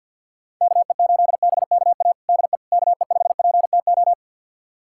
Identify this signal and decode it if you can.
Morse code, heard as a keyed tone, about 46 wpm, 700 Hz. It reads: KE8BKA DE KE5PTO
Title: USA Callsign DE Callsign - 6 Characters - 50wpm